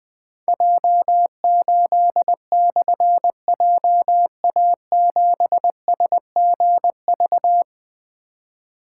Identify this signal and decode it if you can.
Morse code, heard as a keyed tone, about 21 wpm, 700 Hz. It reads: J8/JA7SGV
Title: Difficult Call Signs 20wpm